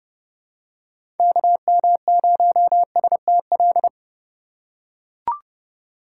Morse code, heard as a keyed tone, about 30 wpm, 700 Hz. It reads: KM0STL
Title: USA Call Signs - 6 Characters - 3x Character Spacing - 30wpm